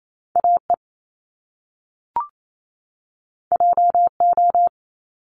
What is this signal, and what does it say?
Morse code, heard as a keyed tone, about 28 wpm, 700 Hz. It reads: AE JO
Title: Top Two Letter Combinations in Words 28wpm